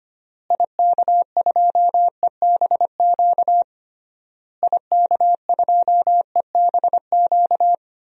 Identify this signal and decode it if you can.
Morse code, heard as a keyed tone, about 25 wpm, 700 Hz. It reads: IK2EBQ IK2EBQ
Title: DXCC-100 Call Signs - 6 Characters - Repeated 3x - 25wpm